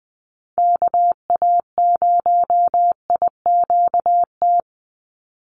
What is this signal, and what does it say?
Morse code, heard as a keyed tone, about 20 wpm, 700 Hz. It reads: KA0IQT
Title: Call Signs - 20wpm